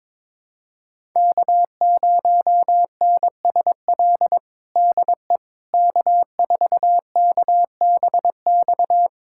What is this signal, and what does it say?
Morse code, heard as a keyed tone, about 22 wpm, 700 Hz. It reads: K0NSL DE K4KBX
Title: USA Callsign DE Callsign - 5 Characters - 22wpm